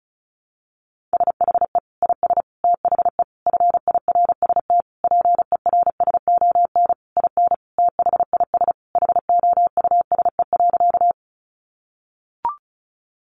Morse code, heard as a keyed tone, about 34 wpm, 700 Hz. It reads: SHE IS THE FIRST PERSON IN THIS HOUSE.
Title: Sentences from Top 400 Words 35wpm